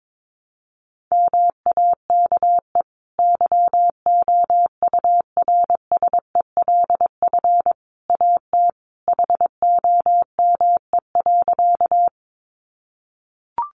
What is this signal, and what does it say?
Morse code, heard as a keyed tone, about 22 wpm, 700 Hz. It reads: MAKE YOURSELF AT HOME.
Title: Sentences from Top 600 Words 22wpm